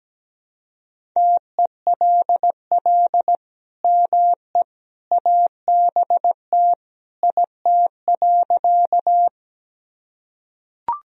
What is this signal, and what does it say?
Morse code, heard as a keyed tone, about 17 wpm, 700 Hz. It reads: TELL ME ABT IT.
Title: Sentences from Top 200 Words and Abbreviations 17wpm